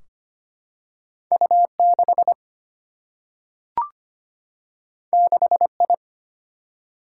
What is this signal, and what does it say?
Morse code, heard as a keyed tone, about 25 wpm, 700 Hz. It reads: U 6 6 I
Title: Single-Letter Single-Number 25wpm